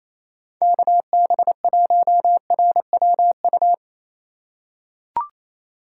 Morse code, heard as a keyed tone, about 28 wpm, 700 Hz. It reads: KB1RWU
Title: USA Call Signs - 6 Characters - 4x Character Spacing - 28wpm